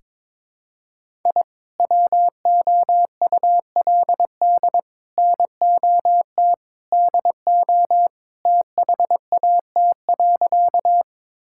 Morse code, heard as a keyed tone, about 22 wpm, 700 Hz. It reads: I WOULD NOT DO THAT.
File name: practice / 22wpm / Sentences with 3-Word Phrases - 22wpm